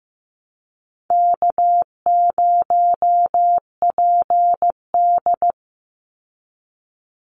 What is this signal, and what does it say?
Morse code, heard as a keyed tone, about 15 wpm, 700 Hz. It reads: K0PD
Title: Easy Call Signs Repeated 3x - 15wpm